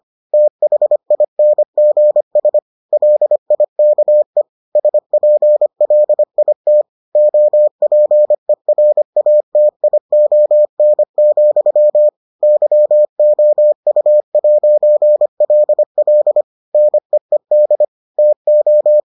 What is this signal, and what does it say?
Morse code, heard as a keyed tone, about 25 wpm, 600 Hz. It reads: THINGS LIKE SPLIT OPERATION, YOU'LL NEED TO